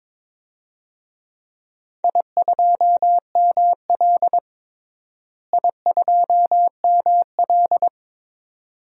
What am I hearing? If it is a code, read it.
Morse code, heard as a keyed tone, about 22 wpm, 700 Hz. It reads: I2ML I2ML
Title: DXCC-100 Call Signs - 4 Characters - Repeated 2x - 22wpm